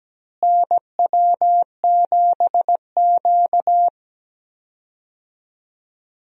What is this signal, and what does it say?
Morse code, heard as a keyed tone, about 17 wpm, 700 Hz. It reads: NW7Q
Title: USA Call Signs - 4 Characters - Repeated 3x - 17wpm